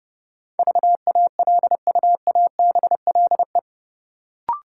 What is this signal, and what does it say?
Morse code, heard as a keyed tone, about 30 wpm, 700 Hz. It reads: VALUABLE
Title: Sentences from Top 1100 Words - New Word Drill Alt - 30wpm